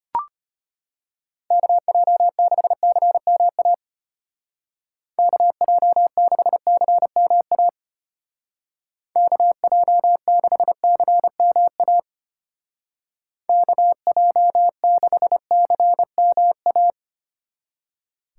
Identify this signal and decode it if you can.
Morse code, heard as a keyed tone, about 32 wpm, 700 Hz. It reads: KJ6CMA KJ6CMA KJ6CMA KJ6CMA